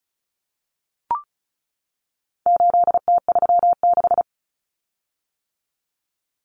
Morse code, heard as a keyed tone, about 34 wpm, 700 Hz. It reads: 8T36
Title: Common Cut Numbers - 4-Digit 35wpm